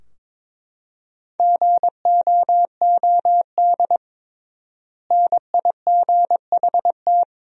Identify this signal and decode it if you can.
Morse code, heard as a keyed tone, about 22 wpm, 700 Hz. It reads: GOOD NIGHT
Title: 3x Word-Spacing - Sentences from Top 300 Words - 22wpm